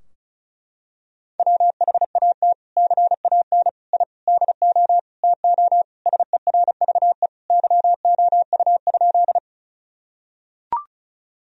Morse code, heard as a keyed tone, about 36 wpm, 700 Hz. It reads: WHAT CAN I DO TO SERVE YOU?
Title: Sentences from Top 1000 Words - Review Part 2 - 35wpm